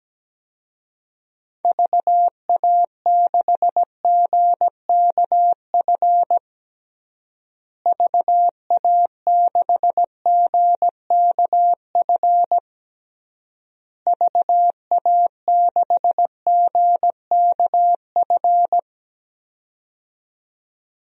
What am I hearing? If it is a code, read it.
Morse code, heard as a keyed tone, about 17 wpm, 700 Hz. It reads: VA6GKF VA6GKF VA6GKF
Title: DXCC-100 Call Signs - 6 Characters - Repeated 3x - 17wpm